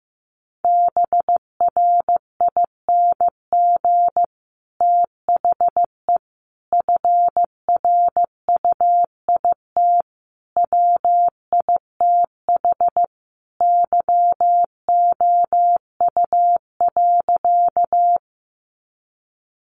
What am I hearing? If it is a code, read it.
Morse code, heard as a keyed tone, about 15 wpm, 700 Hz. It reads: BRING THE FRUIT WITH YOU.
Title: Sentences from Top 1300 Words 15wpm